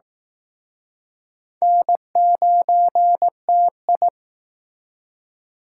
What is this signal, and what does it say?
Morse code, heard as a keyed tone, about 19 wpm, 700 Hz. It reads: N9TI